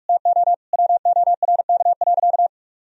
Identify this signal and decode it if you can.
Morse code, heard as a keyed tone, about 42 wpm, 700 Hz. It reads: TO WORK.
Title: Sentences from Top 1000 Words - Review Part 1 - 45wpm